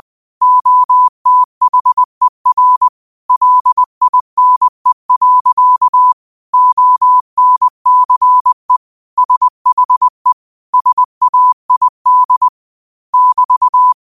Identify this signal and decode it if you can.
Morse code, heard as a keyed tone, about 20 wpm, 1000 Hz. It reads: OTHER LINE. ONCE SHE SAID <BT>